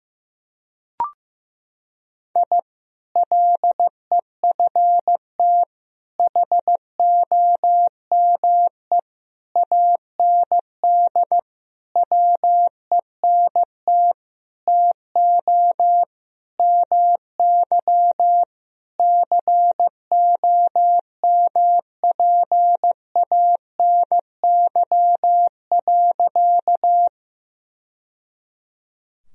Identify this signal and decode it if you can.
Morse code, heard as a keyed tone, about 16 wpm, 700 Hz. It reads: I LEFT HOME AND WENT TO MY COMPANY.
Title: Sentences from Top 1000 Words - Review Part 3 - 15wpm